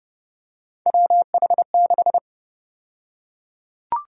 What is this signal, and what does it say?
Morse code, heard as a keyed tone, about 30 wpm, 700 Hz. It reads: WH6
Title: USA Call Sign Prefixes - Repeated 2x - 30wpm